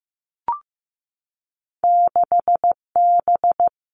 Morse code, heard as a keyed tone, about 15 wpm, 700 Hz. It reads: 6B
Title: Two Challenging Letter-Number Combinations 15wpm